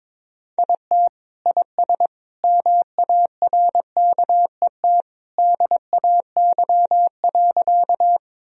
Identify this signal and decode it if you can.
Morse code, heard as a keyed tone, about 22 wpm, 700 Hz. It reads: IT IS MARKET DAY.
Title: Sentences from Top 1000 Words - Review Part 3 - 22wpm